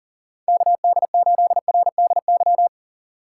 Morse code, heard as a keyed tone, about 40 wpm, 700 Hz. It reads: KD8RDY
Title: USA Call Signs - 6 Characters - 3x Character Spacing - 40wpm